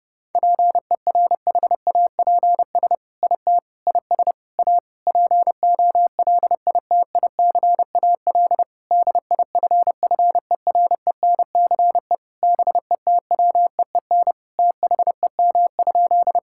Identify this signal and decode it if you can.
Morse code, heard as a keyed tone, about 32 wpm, 700 Hz. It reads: PERHAPS IT IS A POLITICAL DIFFERENCE BETWEEN THEM?
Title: Sentences from Top 1000 Words - Review Part 5 - 30wpm